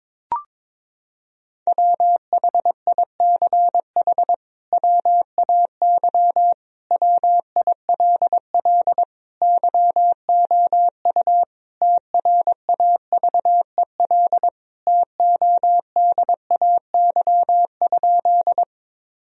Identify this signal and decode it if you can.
Morse code, heard as a keyed tone, about 22 wpm, 700 Hz. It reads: WHICH WAY WILL YOU TRAVEL TODAY?
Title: Sentences from Top 1000 Words - Review Part 2 - 22wpm